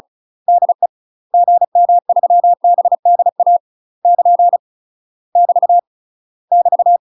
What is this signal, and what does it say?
Morse code, heard as a keyed tone, about 36 wpm, 700 Hz. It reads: DE GM3BDA <KN> <BT> <BT>
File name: UK MASTER QSOs TRAINING / 35WPM / UK MASTER QSOs 0035